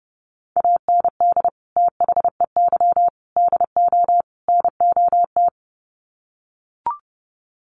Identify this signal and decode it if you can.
Morse code, heard as a keyed tone, about 30 wpm, 700 Hz. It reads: AND THEY DO NOT
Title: Sets of 4 Words 30wpm